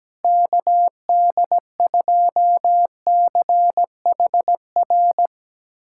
Morse code, heard as a keyed tone, about 17 wpm, 700 Hz. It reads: KD2CHR
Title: USA Callsign DE Callsign - 6 Characters - 17wpm